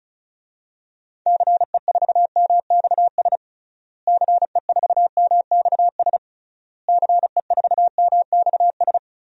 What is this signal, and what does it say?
Morse code, heard as a keyed tone, about 34 wpm, 700 Hz. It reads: CE4MXS CE4MXS CE4MXS
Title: DXCC-100 Call Signs - 6 Characters - Repeated 3x - 35wpm